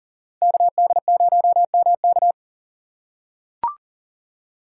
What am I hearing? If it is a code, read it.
Morse code, heard as a keyed tone, about 40 wpm, 700 Hz. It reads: KD0MK
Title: USA Call Signs - 2x Character Spacing - 40wpm